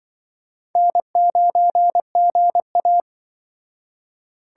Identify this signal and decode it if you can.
Morse code, heard as a keyed tone, about 23 wpm, 700 Hz. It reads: N9GA
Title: Speed Racing - USA 1x2 Call Signs 20wpm